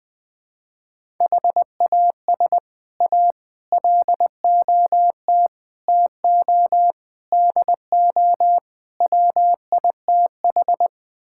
Morse code, heard as a keyed tone, about 20 wpm, 700 Hz. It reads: HAS A LOT TO DO WITH
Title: Sets of 6 Words - 20wpm